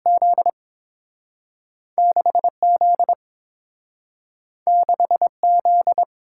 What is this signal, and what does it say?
Morse code, heard as a keyed tone, about 30 wpm, 700 Hz. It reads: Z 6Z 6Z